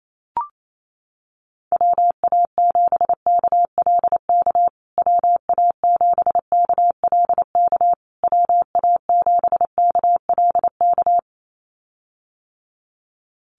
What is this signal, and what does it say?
Morse code, heard as a keyed tone, about 28 wpm, 700 Hz. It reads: WA7KLK WA7KLK WA7KLK
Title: USA Call Signs Repeated 3x 28wpm